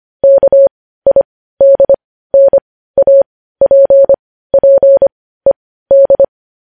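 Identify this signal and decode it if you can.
Morse code, heard as a keyed tone, about 25 wpm, spaced as for 15 wpm, 550 Hz. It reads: KIDNAPPED
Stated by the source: News headlines sent in Morse code at 15 wpm.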